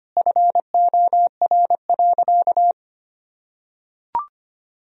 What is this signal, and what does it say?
Morse code, heard as a keyed tone, about 25 wpm, 690 Hz. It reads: FOR.
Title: Sentences from Top 1000 Words - Review Part 4 - 25wpm